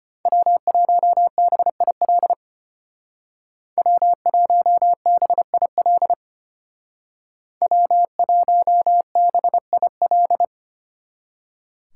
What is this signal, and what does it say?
Morse code, heard as a keyed tone, about 31 wpm, 700 Hz. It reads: W1BIL W1BIL W1BIL
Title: Speed Racing - USA 1x3 Call Signs 25wpm